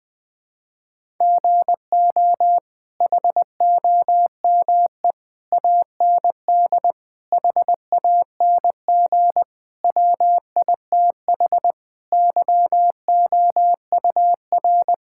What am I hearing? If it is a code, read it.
Morse code, heard as a keyed tone, about 21 wpm, 700 Hz. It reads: GO HOME AND HANG WITH YOUR
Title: Sentences from Top 1600 Words 20wpm